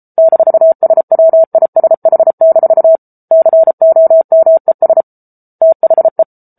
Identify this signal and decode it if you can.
Morse code, heard as a keyed tone, about 34 wpm, 650 Hz. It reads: -SWISH- COMES THE
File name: Book Two / 07-Chapter 7 - 33 WPM